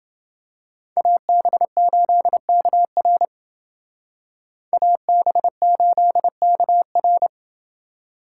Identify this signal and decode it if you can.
Morse code, heard as a keyed tone, about 28 wpm, 700 Hz. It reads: AB8KR AB8KR